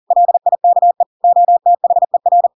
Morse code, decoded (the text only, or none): LIKE OTHER